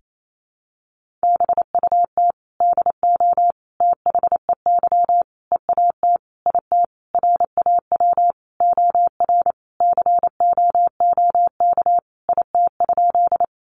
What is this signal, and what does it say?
Morse code, heard as a keyed tone, about 28 wpm, 700 Hz. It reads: BUT DO THEY EAT IT RAW OR COOK IT?
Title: Sentences from Top 1500 Words 28wpm